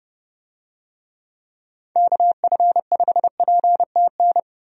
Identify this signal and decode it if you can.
Morse code, heard as a keyed tone, about 30 wpm, 700 Hz. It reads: KF5PTN
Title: USA Call Signs - 6 Characters - 30wpm